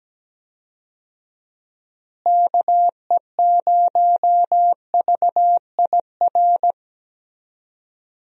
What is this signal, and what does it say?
Morse code, heard as a keyed tone, about 17 wpm, 700 Hz. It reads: KE0VIR
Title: USA Call Signs - 6 Characters - Repeated 3x - 17wpm